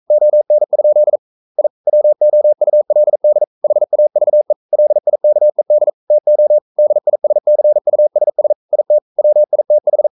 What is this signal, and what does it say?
Morse code, heard as a keyed tone, about 42 wpm, 600 Hz. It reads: ON? I WOULD HAVE LIKED TO DISCUSS IT WITH
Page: Romanen Never af Ken Follet på engelsk.